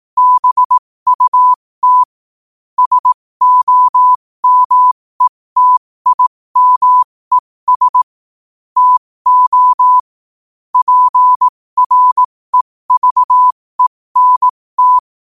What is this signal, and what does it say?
Morse code, heard as a keyed tone, about 18 wpm, spaced as for 15 wpm, 1000 Hz. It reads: BUT SOMETIMES TO PREVENT
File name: qotdfiles / 15wpm